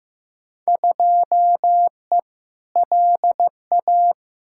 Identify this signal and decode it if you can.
Morse code, heard as a keyed tone, about 15 wpm, 700 Hz. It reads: 2E LA
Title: Field Day Exchanges Encore - 15wpm